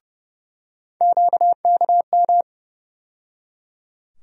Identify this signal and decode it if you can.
Morse code, heard as a keyed tone, about 30 wpm, 700 Hz. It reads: QKM
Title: USA Call Sign Postfixes - 3 Letters 30wpm